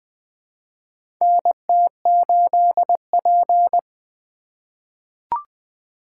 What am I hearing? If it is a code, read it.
Morse code, heard as a keyed tone, about 20 wpm, 700 Hz. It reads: NT8P
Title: USA Call Signs - 8x Character Spacing - 20wpm